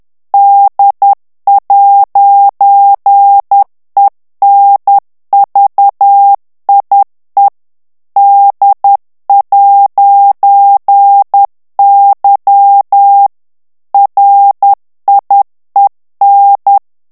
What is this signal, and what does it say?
Morse code, heard as a keyed tone, about 10 wpm, 800 Hz. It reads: D'ENVIE D'Y RIEN
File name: FRANCAIS TEXTE CLAIR 2 VITESSE 10 MOTS MINUTES